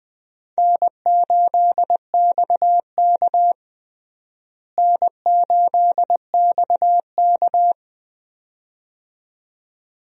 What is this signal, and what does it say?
Morse code, heard as a keyed tone, about 20 wpm, 700 Hz. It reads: N8XK N8XK
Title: USA Call Signs - 4 Characters - Repeated 2x - 20wpm